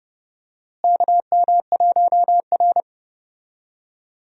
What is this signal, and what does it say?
Morse code, heard as a keyed tone, about 30 wpm, 700 Hz. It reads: KM1R
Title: USA Call Signs - 4 Characters - Repeated 2x - 30wpm